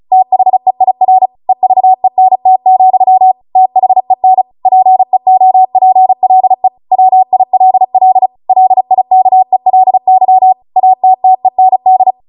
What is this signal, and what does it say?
Morse code, heard as a keyed tone, about 36 wpm, 750 Hz. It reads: THEIR EVENT, THEN PEOPLE WILL LIKELY ATTEND